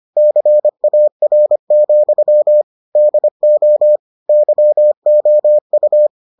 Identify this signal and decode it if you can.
Morse code, heard as a keyed tone, about 25 wpm, 600 Hz. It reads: CAR, DO YOU